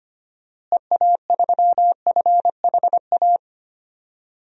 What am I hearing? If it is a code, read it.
Morse code, heard as a keyed tone, about 25 wpm, 700 Hz. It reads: EA3FHA
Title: DXCC-100 Call Signs - 6 Characters - Repeated 3x - 25wpm